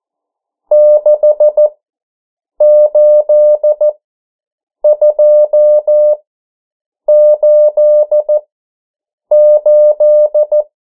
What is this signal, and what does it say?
Morse code, heard as a keyed tone, about 14 wpm, spaced as for 7 wpm, 600 Hz. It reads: 68288
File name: morse mp3 / 09 urok 8